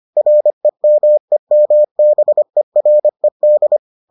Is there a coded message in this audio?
Yes: Morse code from 0.2 to 3.8 s, heard as a keyed tone, about 25 wpm, 600 Hz. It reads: REMEMBERED